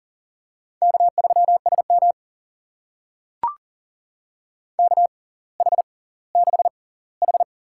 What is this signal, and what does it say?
Morse code, heard as a keyed tone, about 38 wpm, spaced as for 20 wpm, 690 Hz. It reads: K3SM KH6H
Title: USA Call Signs - 4 Characters - 6x Character Spacing - 40wpm